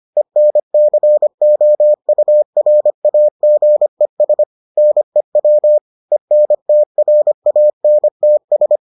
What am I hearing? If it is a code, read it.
Morse code, heard as a keyed tone, about 25 wpm, 600 Hz. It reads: ENCOURAGES NEW ENTRANTS